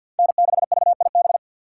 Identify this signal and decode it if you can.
Morse code, heard as a keyed tone, about 51 wpm, 700 Hz. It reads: N6VIB
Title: USA Call Signs - 5 Characters - 3x Character Spacing - 50wpm